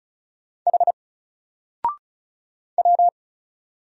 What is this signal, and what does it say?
Morse code, heard as a keyed tone, about 33 wpm, 700 Hz. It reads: H W
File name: Single-Letter-Number-35wpm